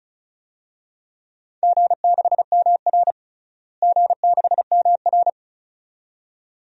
Morse code, heard as a keyed tone, about 36 wpm, 700 Hz. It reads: G6MR G6MR
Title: DXCC-100 Call Signs - 4 Characters - Repeated 2x - 35wpm